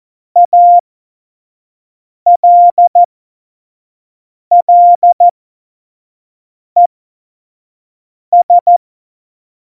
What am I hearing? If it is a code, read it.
Morse code, heard as a keyed tone, about 14 wpm, spaced as for 5 wpm, 700 Hz. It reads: ALLES
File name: Morsekurs / Witze / Witz 06. Februar 2018, 25 F70